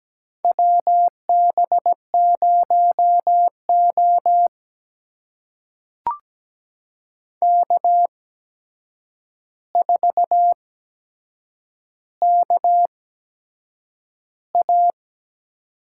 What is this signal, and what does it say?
Morse code, heard as a keyed tone, about 17 wpm, spaced as for 7 wpm, 700 Hz. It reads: WB0O K4KA
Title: USA Call Signs - 4 Characters - 8x Character Spacing - 17wpm